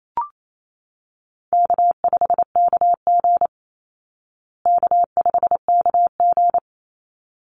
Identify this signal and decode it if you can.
Morse code, heard as a keyed tone, about 28 wpm, 700 Hz. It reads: K5KG K5KG
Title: Easy Call Signs Repeated 2x - 28wpm